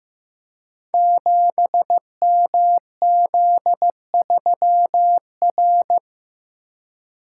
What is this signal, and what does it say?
Morse code, heard as a keyed tone, about 15 wpm, 700 Hz. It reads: 7MZ3R
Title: Code Groups - 5 Letter:Numbers - 15wpm